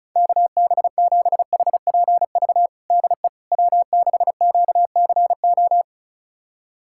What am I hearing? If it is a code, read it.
Morse code, heard as a keyed tone, about 36 wpm, 700 Hz. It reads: KB7HPV DE W6QCO
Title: USA Callsign DE Callsign 35wpm